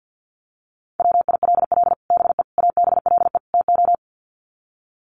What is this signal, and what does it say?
Morse code, heard as a keyed tone, about 50 wpm, 700 Hz. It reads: WILL BE ABLE TO
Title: Sets of 4 Words 50wpm